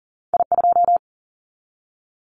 Morse code, heard as a keyed tone, about 42 wpm, 700 Hz. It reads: I2
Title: Callsign Prefixes for Top 50 DXCC - 40wpm